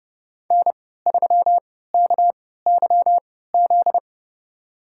Morse code, heard as a keyed tone, about 32 wpm, spaced as for 17 wpm, 700 Hz. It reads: N3KYZ
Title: USA Call Signs - 5 Characters - 3x Character Spacing - 30wpm